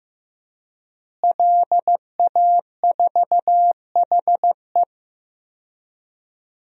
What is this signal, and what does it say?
Morse code, heard as a keyed tone, about 16 wpm, 700 Hz. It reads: LA4HE
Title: DXCC-100 Call Signs - 5 Characters - Repeated 3x - 15wpm